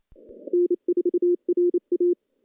Morse code, heard as a keyed tone, about 28 wpm, 350 Hz. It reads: N4RA
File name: contests / CW-SS-k5tr- -0100